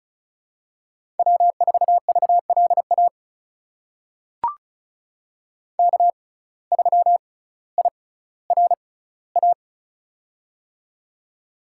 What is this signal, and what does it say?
Morse code, heard as a keyed tone, about 34 wpm, spaced as for 18 wpm, 700 Hz. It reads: W4VLA K3IRA
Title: USA Call Signs - 5 Characters - 6x Character Spacing - 35wpm